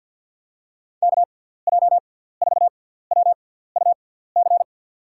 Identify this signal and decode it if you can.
Morse code, heard as a keyed tone, about 53 wpm, spaced as for 17 wpm, 700 Hz. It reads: KJ4WUC
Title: USA Call Signs - 6x Character Spacing - 50wpm